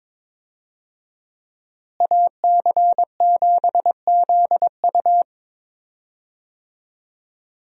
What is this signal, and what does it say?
Morse code, heard as a keyed tone, about 22 wpm, 700 Hz. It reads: AC7ZU
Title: USA Call Signs - 5 Characters - Repeated 3x - 22wpm